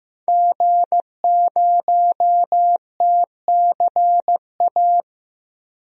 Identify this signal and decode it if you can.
Morse code, heard as a keyed tone, about 15 wpm, 700 Hz. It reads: G0TCA
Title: DXCC-100 Call Signs - 5 Characters - 15wpm